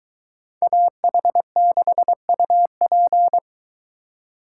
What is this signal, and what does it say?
Morse code, heard as a keyed tone, about 22 wpm, 700 Hz. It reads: AH6UP